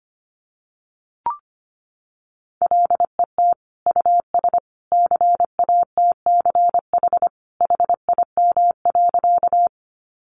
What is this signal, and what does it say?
Morse code, heard as a keyed tone, about 25 wpm, 700 Hz. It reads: LET US CATCH HIM.
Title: Sentences from Top 1500 Words 25wpm